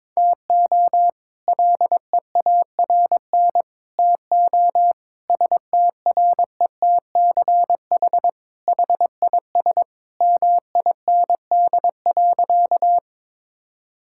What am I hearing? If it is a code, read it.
Morse code, heard as a keyed tone, about 22 wpm, 700 Hz. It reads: TO LEARN TO STRETCH HIS MIND.
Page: morse-code-ninja-practice-22wpm